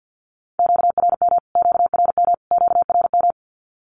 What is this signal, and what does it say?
Morse code, heard as a keyed tone, about 50 wpm, 700 Hz. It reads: QRM QRM QRM
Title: Q-Codes Repeated 3x 50wpm